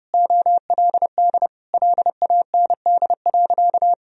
Morse code, heard as a keyed tone, about 30 wpm, 700 Hz. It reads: OLD LAND.
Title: Sentences from Top 1000 Words - Review Part 1 - 30wpm